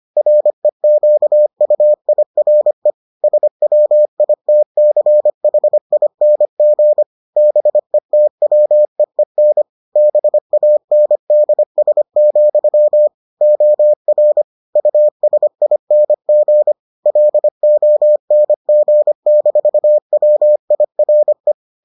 Morse code, heard as a keyed tone, about 25 wpm, 600 Hz. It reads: REQUIRE SWITCHING BETWEEN BANDS, OR USING LONG-WIRE